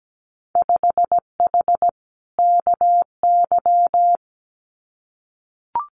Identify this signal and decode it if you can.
Morse code, heard as a keyed tone, about 17 wpm, 700 Hz. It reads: 5H KY
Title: Winter Field Day Exchanges Encore 17wpm